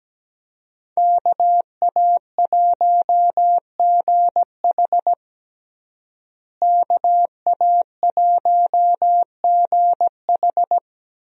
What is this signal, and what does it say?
Morse code, heard as a keyed tone, about 17 wpm, 700 Hz. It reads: KA1GH KA1GH
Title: USA Call Signs - 5 Characters - Repeated 3x - 17wpm